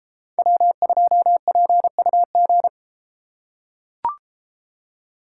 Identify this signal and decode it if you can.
Morse code, heard as a keyed tone, about 32 wpm, 700 Hz. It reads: W2PUG